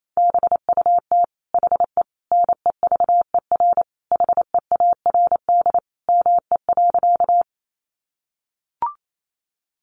Morse code, heard as a keyed tone, about 28 wpm, 700 Hz. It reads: BUT HE NEVER HEARD ME.
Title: Sentences from Top 400 Words 28wpm